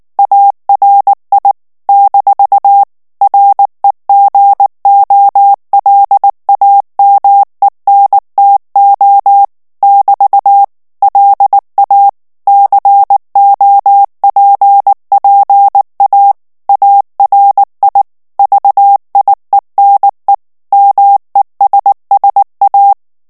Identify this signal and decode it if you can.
Morse code, heard as a keyed tone, about 19 wpm, 800 Hz. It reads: ARI - REGOLAMENTO <BT> LA COPPA ARI VIENE MESSA
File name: morse mp3 / ITALIEN texte clair 2 vitesse 18 mots minute